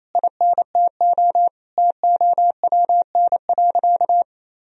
Morse code, heard as a keyed tone, about 27 wpm, 700 Hz. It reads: INTO TOWN.
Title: Sentences from Top 1000 Words - Review Part 1 - 28wpm